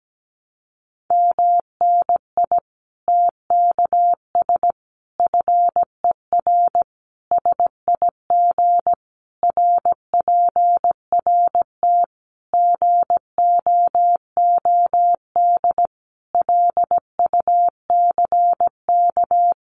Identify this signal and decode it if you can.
Morse code, heard as a keyed tone, about 17 wpm, 700 Hz. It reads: MNI TKS FER SIG RPRT GOOD LUCK
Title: Common QSO Phrases 17wpm